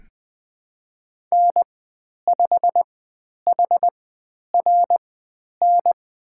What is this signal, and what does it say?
Morse code, heard as a keyed tone, about 20 wpm, spaced as for 10 wpm, 700 Hz. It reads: N5HRN
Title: 095 Lesson 16g - Letter c - Call Signs 20wpm